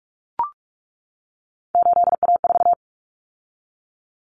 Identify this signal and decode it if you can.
Morse code, heard as a keyed tone, about 44 wpm, 700 Hz. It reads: 8A4
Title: Common Cut Numbers - 3-Digit 45wpm